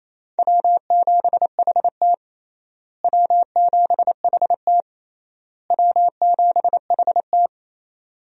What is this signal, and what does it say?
Morse code, heard as a keyed tone, about 28 wpm, 700 Hz. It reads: W7HT W7HT W7HT
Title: USA Call Signs - 4 Characters - Repeated 3x - 28wpm